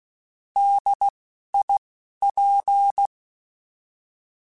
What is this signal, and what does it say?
Morse code, heard as a keyed tone, about 16 wpm, 800 Hz. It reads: D I P
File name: Alphabetical / Alphabetical mp3s / Ds @10